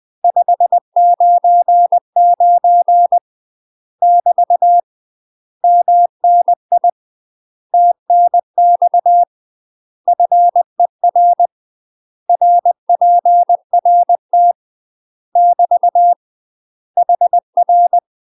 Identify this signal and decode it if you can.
Morse code, heard as a keyed tone, about 20 wpm, 700 Hz. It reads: 599 <BT> MNI TNX FER RPRT <BT> HR
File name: UK MASTER QSOs TRAINING / 20WPM / UK MASTER QSOs 0041